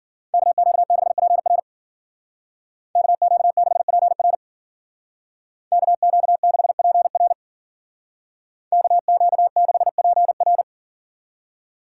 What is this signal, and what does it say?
Morse code, heard as a keyed tone, about 52 wpm, 700 Hz. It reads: KQ6PR KQ6PR KQ6PR KQ6PR